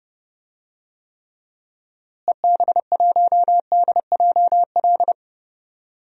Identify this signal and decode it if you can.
Morse code, heard as a keyed tone, about 32 wpm, 700 Hz. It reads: EB1DJL
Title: DXCC-100 Call Signs - 6 Characters - Repeated 2x - 30wpm